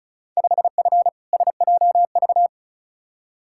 Morse code, heard as a keyed tone, about 36 wpm, 700 Hz. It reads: 5F SJV
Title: Field Day Exchanges Encore - 35wpm